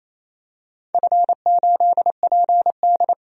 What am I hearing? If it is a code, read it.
Morse code, heard as a keyed tone, about 28 wpm, 700 Hz. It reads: F8PD
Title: DXCC-100 Call Signs - 4 Characters - 28wpm